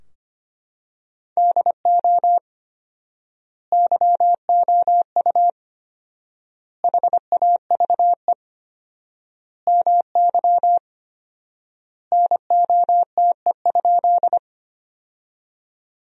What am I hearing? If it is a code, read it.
Morse code, heard as a keyed tone, about 25 wpm, 700 Hz. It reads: DO YOU HAVE MY NOTE?
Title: 4x Word-Spacing - Sentences from Top 500 Words - 25wpm